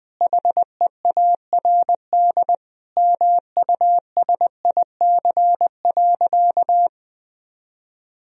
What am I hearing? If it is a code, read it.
Morse code, heard as a keyed tone, about 20 wpm, 700 Hz. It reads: HEARD MUSIC.
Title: Sentences from Top 1000 Words - Review Part 1 - 20wpm